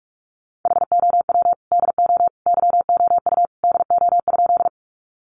Morse code, heard as a keyed tone, about 44 wpm, 700 Hz. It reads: HOW DO YOU DO?
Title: Sentences from Top 100 Words 45wpm